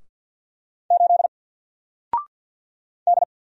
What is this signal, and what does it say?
Morse code, heard as a keyed tone, about 48 wpm, 700 Hz. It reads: 8 D
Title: Single Letter-Number 50wpm